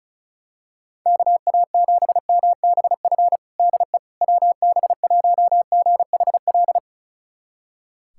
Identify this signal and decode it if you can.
Morse code, heard as a keyed tone, about 34 wpm, 700 Hz. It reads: KA7MBF DE WB1GHL
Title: USA Callsign DE Callsign - 6 Characters - 35wpm